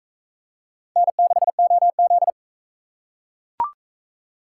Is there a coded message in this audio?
Yes: Morse code from 1.0 to 2.3 s, heard as a keyed tone, about 40 wpm, 700 Hz. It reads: N6OZ